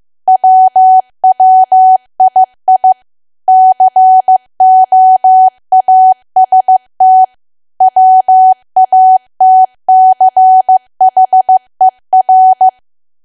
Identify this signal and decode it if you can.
Morse code, heard as a keyed tone, about 15 wpm, 750 Hz. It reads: WWII COAST WATCHER
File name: cwa / W1AW-Level2 / 15WPM